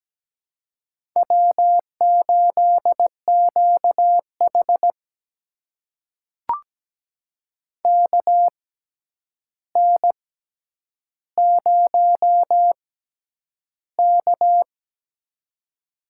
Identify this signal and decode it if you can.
Morse code, heard as a keyed tone, about 17 wpm, spaced as for 9 wpm, 700 Hz. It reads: W8QH KN0K
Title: USA Call Signs - 4 Characters - 6x Character Spacing - 17wpm